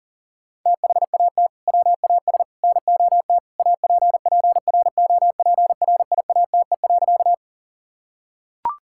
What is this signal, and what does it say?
Morse code, heard as a keyed tone, about 40 wpm, 700 Hz. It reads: THAT WAS NOT APPROPRIATE.
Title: Sentences from Top 1000 Words - Review Part 5 - 40wpm